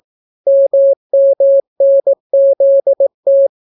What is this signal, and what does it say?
Morse code, heard as a keyed tone, about 18 wpm, 550 Hz. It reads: MMNZT